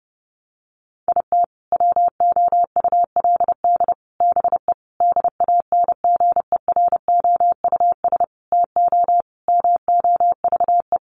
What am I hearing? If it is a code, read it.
Morse code, heard as a keyed tone, about 30 wpm, 700 Hz. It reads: IT WOULD BE DANGEROUS TO MOVE
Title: Sentences from Top 1200 Words 30wpm